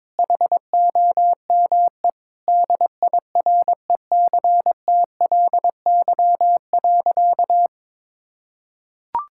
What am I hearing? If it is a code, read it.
Morse code, heard as a keyed tone, about 22 wpm, 700 Hz. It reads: HOME DIRECTLY.
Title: Sentences from Top 1000 Words - Review Part 5 - 22wpm